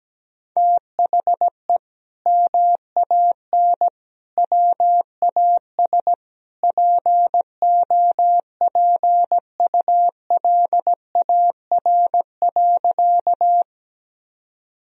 Morse code, heard as a keyed tone, about 17 wpm, 700 Hz. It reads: THE MAN WAS POPULAR.
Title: Sentences from Top 1000 Words - Review Part 3 - 17wpm